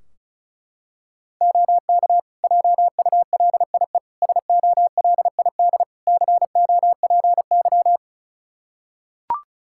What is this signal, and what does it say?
Morse code, heard as a keyed tone, about 34 wpm, 700 Hz. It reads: OK JULIE SOLID COPY
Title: Practice QSO - Exchange 3 - More Info - 35wpm